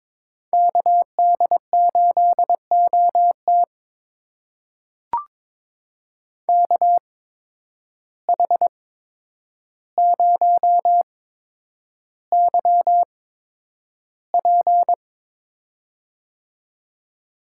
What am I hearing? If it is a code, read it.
Morse code, heard as a keyed tone, about 22 wpm, spaced as for 9 wpm, 700 Hz. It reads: KD8OT KH0YP
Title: USA Call Signs - 5 Characters - 8x Character Spacing - 22wpm